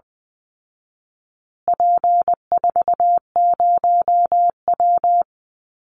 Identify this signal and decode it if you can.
Morse code, heard as a keyed tone, about 20 wpm, 700 Hz. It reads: P40W
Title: Call Signs Repeated 2x - 20wpm